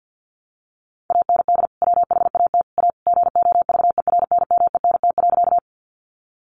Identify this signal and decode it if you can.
Morse code, heard as a keyed tone, about 50 wpm, 700 Hz. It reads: AND WHAT A GOVERNMENT.
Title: Sentences from Top 600 Words 50wpm